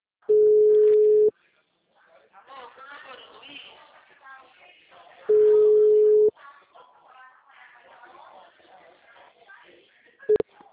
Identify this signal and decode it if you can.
Ringback tone (Europe, 425 Hz), with speech.